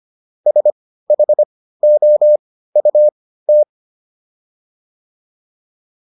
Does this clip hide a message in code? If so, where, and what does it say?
Yes: Morse code from 0.5 to 3.6 s, heard as a keyed tone, about 25 wpm, spaced as for 15 wpm, 600 Hz. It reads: SHOUT